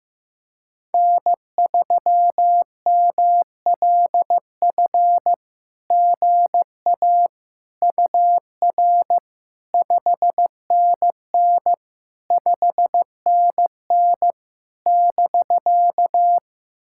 Morse code, heard as a keyed tone, about 15 wpm, 700 Hz. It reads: N3MLF GA UR 5NN 5NN <BK>
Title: SOTA QSO Practice 15wpm